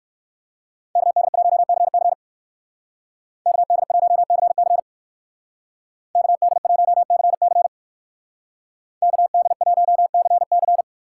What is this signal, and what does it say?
Morse code, heard as a keyed tone, about 59 wpm, 700 Hz. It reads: KD1CC KD1CC KD1CC KD1CC